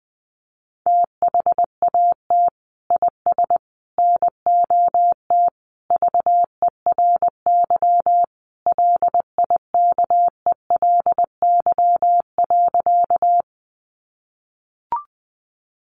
Morse code, heard as a keyed tone, about 20 wpm, 700 Hz. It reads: THAT IS NOT VERY LIKELY.
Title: Sentences from Top 600 Words 20wpm